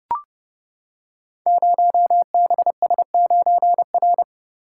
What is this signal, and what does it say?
Morse code, heard as a keyed tone, about 29 wpm, 700 Hz. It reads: 0BS9R
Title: Code Groups - 5 Letter:Numbers - 30wpm